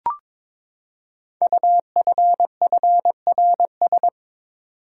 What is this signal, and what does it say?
Morse code, heard as a keyed tone, about 22 wpm, 700 Hz. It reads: UFFRS
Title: Code Groups - 5 Letters - 22wpm